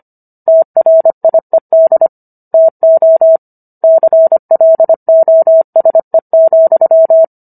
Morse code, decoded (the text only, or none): TRIED TO CLOSE,